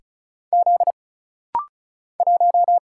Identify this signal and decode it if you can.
Morse code, heard as a keyed tone, about 36 wpm, 700 Hz. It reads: Z 1
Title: Single Letter-Number - Rapid-Fire 35wpm